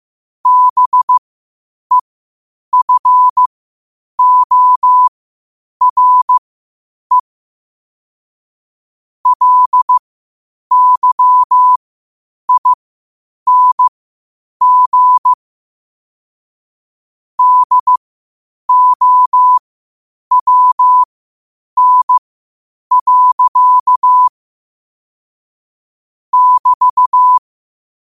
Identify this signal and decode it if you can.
Morse code, heard as a keyed tone, about 15 wpm, spaced as for 8 wpm, 1000 Hz. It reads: BEFORE LYING DOWN. <BT>
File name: qotdfiles / 8wpm